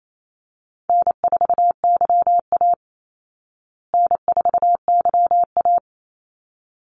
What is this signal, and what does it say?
Morse code, heard as a keyed tone, about 28 wpm, 700 Hz. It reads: N4YA N4YA
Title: Easy Call Signs Repeated 2x - 28wpm